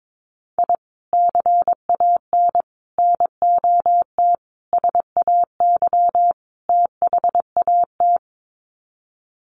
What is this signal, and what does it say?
Morse code, heard as a keyed tone, about 22 wpm, 700 Hz. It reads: I CAN NOT SAY THAT
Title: Sets of 5 Words Encore2 22wpm